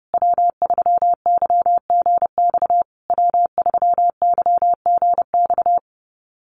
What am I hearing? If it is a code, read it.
Morse code, heard as a keyed tone, about 28 wpm, 700 Hz. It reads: W3YGX W3YGX
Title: USA Call Signs Repeated 3x 30wpm